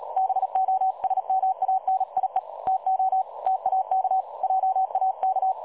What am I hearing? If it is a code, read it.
Morse code, heard as a keyed tone, about 37 wpm, 760 Hz. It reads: BOSMANIE TO TAK JAK
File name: dlazaawansowanych / nagraniazpasma / websdr recording 3565.0kHz